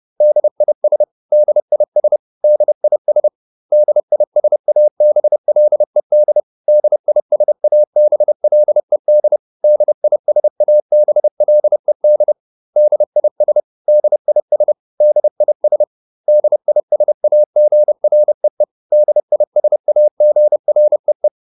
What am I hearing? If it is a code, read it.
Morse code, heard as a keyed tone, about 30 wpm, 600 Hz. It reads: DIS DIS DIS DISABLED DISABLED DISABLED DIS DIS DIS DISAGREE DISAGREE